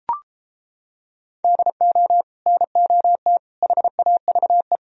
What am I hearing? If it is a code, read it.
Morse code, heard as a keyed tone, about 33 wpm, 700 Hz. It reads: DO NOT HAVE